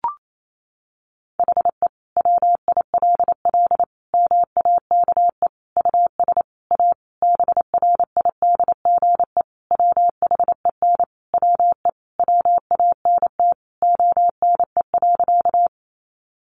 Morse code, heard as a keyed tone, about 28 wpm, 700 Hz. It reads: HE WILL MAKE US A BRIDGE WHEN WE WANT ONE.
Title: Sentences from Top 1200 Words 28wpm